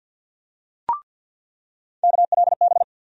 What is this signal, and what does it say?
Morse code, heard as a keyed tone, about 50 wpm, 700 Hz. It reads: KLB
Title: USA Call Sign Postfixes - 3 Letters 50wpm